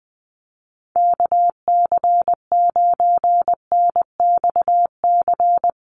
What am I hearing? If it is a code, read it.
Morse code, heard as a keyed tone, about 19 wpm, 700 Hz. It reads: KC9NXC
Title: USA Call Signs 20wpm